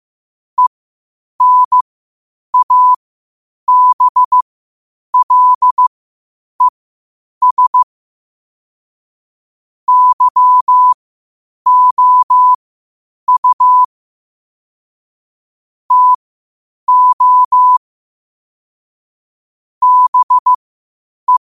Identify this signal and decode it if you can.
Morse code, heard as a keyed tone, about 15 wpm, spaced as for 8 wpm, 1000 Hz. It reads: ENABLES YOU TO BE
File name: qotdfiles / 8wpm